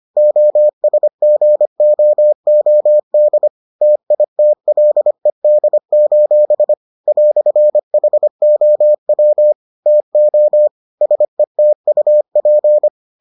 Morse code, heard as a keyed tone, about 25 wpm, 600 Hz. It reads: OSGOOD TITLED: "HOW TO SETUP